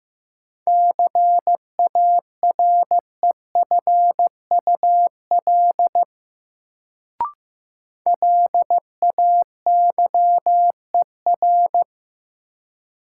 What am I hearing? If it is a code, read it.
Morse code, heard as a keyed tone, about 15 wpm, 700 Hz. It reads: CAREFUL LAYER
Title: Sentences from Top 1600 Words - New Word Drill Alt - 15wpm